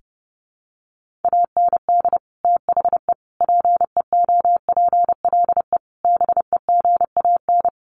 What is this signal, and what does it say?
Morse code, heard as a keyed tone, about 30 wpm, 700 Hz. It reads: AND THE PEOPLE BEGAN
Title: Sentences from Top 1500 Words 30wpm